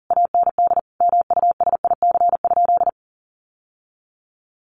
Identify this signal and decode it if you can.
Morse code, heard as a keyed tone, about 40 wpm, 700 Hz. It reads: AND MUSIC?
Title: Sentences from Top 1200 Words 40wpm